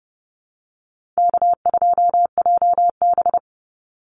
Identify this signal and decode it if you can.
Morse code, heard as a keyed tone, about 30 wpm, 700 Hz. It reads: K2JB
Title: NA SOTA Chaser Callsigns 30wpm